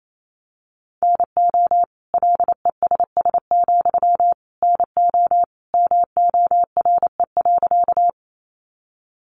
Morse code, heard as a keyed tone, about 28 wpm, 700 Hz. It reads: NO LESS, NO MORE.
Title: Sentences from Top 400 Words 28wpm